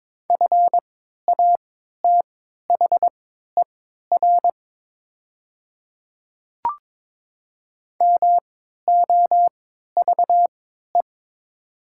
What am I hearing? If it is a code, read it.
Morse code, heard as a keyed tone, about 21 wpm, spaced as for 12 wpm, 700 Hz. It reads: FATHER MOVE
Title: 3x Character Spacing - Top 200 Words - 22wpm